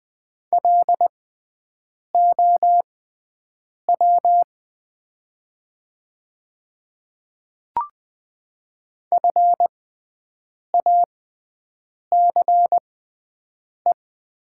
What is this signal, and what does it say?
Morse code, heard as a keyed tone, about 20 wpm, spaced as for 7 wpm, 700 Hz. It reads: LOW FACE
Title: 6x Character Spacing - Top 500 Words - 20wpm